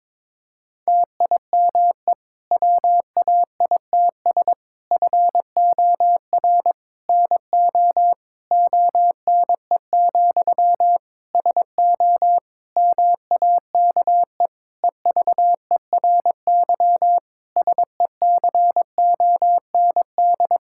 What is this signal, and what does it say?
Morse code, heard as a keyed tone, about 22 wpm, 700 Hz. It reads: TIME WAITS FOR NO ONE, SO MAKE EVERY SECOND
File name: Difficult Sentences with Hints - 22wpm